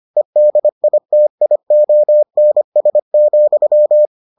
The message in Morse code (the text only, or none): EDITIONS,